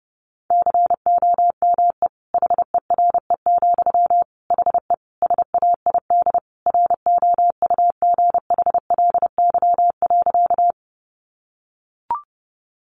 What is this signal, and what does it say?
Morse code, heard as a keyed tone, about 30 wpm, 700 Hz. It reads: COME HERE, HE SAID ROUGHLY.
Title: Sentences from Top 1500 Words 30wpm